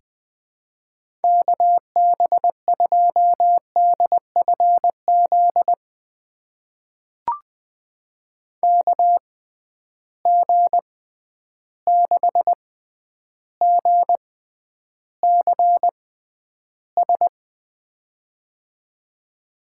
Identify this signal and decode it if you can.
Morse code, heard as a keyed tone, about 20 wpm, spaced as for 10 wpm, 700 Hz. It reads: KB2DFZ KG6GCS
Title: USA Call Signs - 6 Characters - 6x Character Spacing - 20wpm